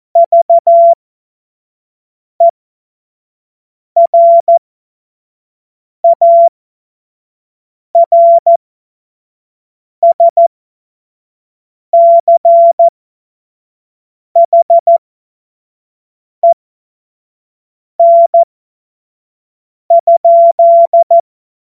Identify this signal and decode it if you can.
Morse code, heard as a keyed tone, about 14 wpm, spaced as for 5 wpm, 680 Hz. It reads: VERARSCHEN?
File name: Morsekurs / Witze / Witz 22. Juni 2021, 25 F70